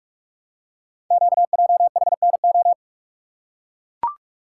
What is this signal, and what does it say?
Morse code, heard as a keyed tone, about 44 wpm, 700 Hz. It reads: QJHNO
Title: Code Groups - 5 Letters - 45wpm